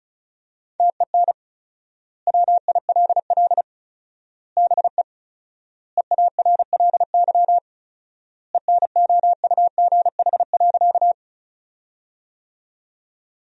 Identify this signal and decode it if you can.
Morse code, heard as a keyed tone, about 36 wpm, 700 Hz. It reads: TEN WILL BE EARLY ENOUGH.
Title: 4x Word-Spacing - Sentences from Top 400 Words - 35wpm